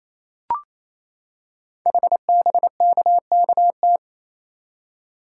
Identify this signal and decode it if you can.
Morse code, heard as a keyed tone, about 27 wpm, 700 Hz. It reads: HBKKT
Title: Code Groups - 5 Letters - 28wpm